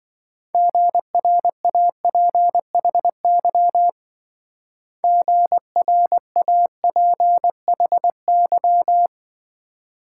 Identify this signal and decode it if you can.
Morse code, heard as a keyed tone, about 25 wpm, 700 Hz. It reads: GRAPHY GRAPHY